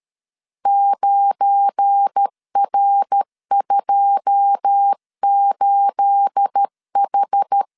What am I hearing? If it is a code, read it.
Morse code, heard as a keyed tone, about 13 wpm, 800 Hz. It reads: 9R28H
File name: CW / DET / 1 9 1500C S B 12wpm 3819